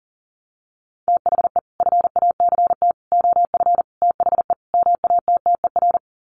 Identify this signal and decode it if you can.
Morse code, heard as a keyed tone, about 39 wpm, 700 Hz. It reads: THE FACT OF THE MATTER
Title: Sets of 5 Words 40wpm